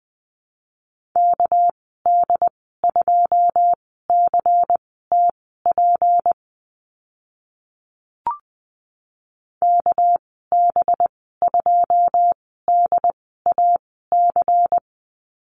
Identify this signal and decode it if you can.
Morse code, heard as a keyed tone, about 20 wpm, spaced as for 14 wpm, 700 Hz. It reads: KD2CTP KB2DAC
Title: 207 Lesson 33g - Letter j - Call Signs 20wpm